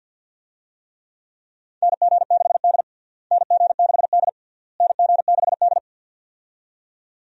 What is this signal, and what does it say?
Morse code, heard as a keyed tone, about 51 wpm, 700 Hz. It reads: NG6D NG6D NG6D
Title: USA Call Signs - 4 Characters - Repeated 3x - 50wpm